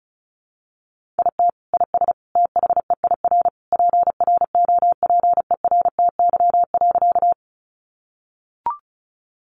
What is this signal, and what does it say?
Morse code, heard as a keyed tone, about 34 wpm, 700 Hz. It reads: IT IS THEIR PROPERTY.
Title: Sentences from Top 1000 Words 35wpm